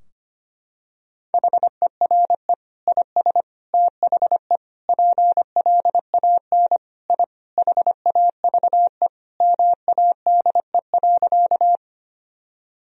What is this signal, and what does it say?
Morse code, heard as a keyed tone, about 25 wpm, 700 Hz. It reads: HERE IS THE PLAN I HAVE MADE.
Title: Sentences from Top 1000 Words - Review Part 2 - 25wpm